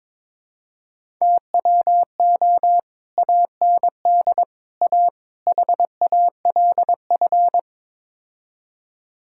Morse code, heard as a keyed tone, about 21 wpm, 700 Hz. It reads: TWO AND A HALF
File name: practice / 22wpm / Build A Phrase - 5-Word Phrases - 22wpm